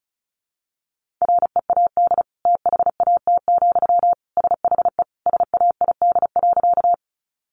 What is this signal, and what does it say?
Morse code, heard as a keyed tone, about 34 wpm, 700 Hz. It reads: READ THAT, SHE SAID.
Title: Sentences from Top 200 Words 35wpm